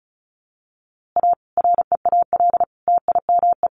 Morse code, heard as a keyed tone, about 34 wpm, 700 Hz. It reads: A REAL TIME
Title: A Sound Of Thunder 35wpm